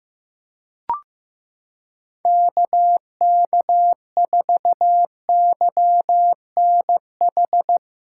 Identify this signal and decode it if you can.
Morse code, heard as a keyed tone, about 15 wpm, 700 Hz. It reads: KK4YNH
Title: USA Callsign DE Callsign - 6 Characters - 15wpm